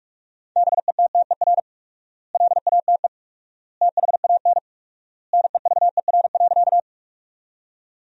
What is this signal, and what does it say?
Morse code, heard as a keyed tone, about 46 wpm, 700 Hz. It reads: BETTER LATE THAN NEVER.
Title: 4x Word-Spacing - Sentences from Top 400 Words - 45wpm